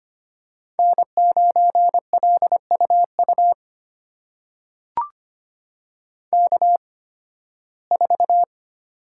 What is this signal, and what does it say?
Morse code, heard as a keyed tone, about 25 wpm, spaced as for 16 wpm, 700 Hz. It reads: N9LUU K4
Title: USA Call Signs - 5 Characters - 8x Character Spacing - 25wpm